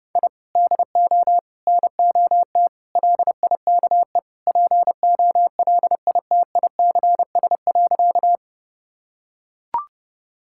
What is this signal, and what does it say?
Morse code, heard as a keyed tone, about 30 wpm, 700 Hz. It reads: I DO NOT LIKE POLITICS.
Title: Sentences from Top 1000 Words - Review Part 6 - 30wpm